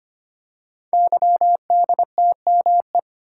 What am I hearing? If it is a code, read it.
Morse code, heard as a keyed tone, about 25 wpm, 700 Hz. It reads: YDTME
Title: Code Groups - 5 Letters - 25wpm